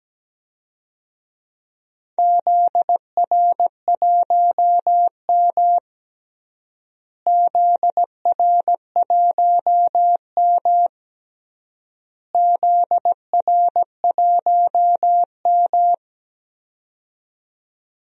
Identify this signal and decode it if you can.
Morse code, heard as a keyed tone, about 17 wpm, 700 Hz. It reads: ZR1M ZR1M ZR1M
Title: DXCC-100 Call Signs - 4 Characters - Repeated 3x - 17wpm